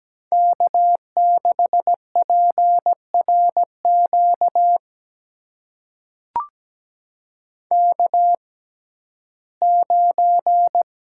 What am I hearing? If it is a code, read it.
Morse code, heard as a keyed tone, about 17 wpm, spaced as for 12 wpm, 700 Hz. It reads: K6PRQ K9
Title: USA Call Signs - 5 Characters - 6x Character Spacing - 17wpm